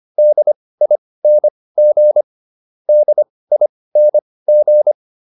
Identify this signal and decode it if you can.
Morse code, heard as a keyed tone, about 25 wpm, spaced as for 18 wpm, 600 Hz. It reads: DING DING